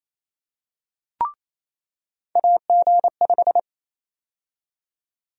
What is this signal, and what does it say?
Morse code, heard as a keyed tone, about 28 wpm, 700 Hz. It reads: AG5
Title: USA Call Sign Prefixes - 28wpm